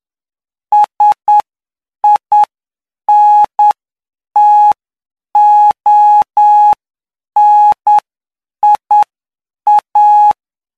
Morse code, heard as a keyed tone, about 9 wpm, spaced as for 7 wpm, 840 Hz. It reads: SINTONIA